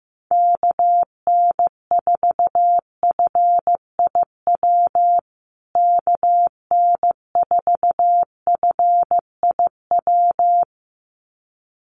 Morse code, heard as a keyed tone, about 14 wpm, 700 Hz. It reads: KN4FIW KN4FIW
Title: USA Call Signs Repeated 2x 15wpm